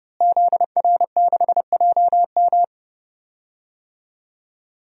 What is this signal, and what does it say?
Morse code, heard as a keyed tone, about 30 wpm, 700 Hz. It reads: ZR6JM
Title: DXCC-100 Call Signs - 5 Characters - Repeated 2x - 30wpm